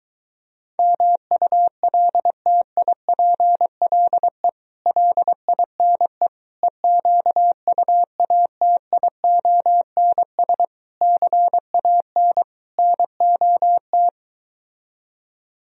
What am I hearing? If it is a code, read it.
Morse code, heard as a keyed tone, about 23 wpm, 700 Hz. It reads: MULTIPLE LINE EQUATIONS CAN NOT